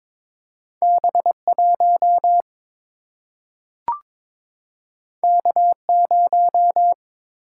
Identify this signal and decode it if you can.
Morse code, heard as a keyed tone, about 22 wpm, 700 Hz. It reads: B 1 K 0
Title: Single-Letter Single-Number 22wpm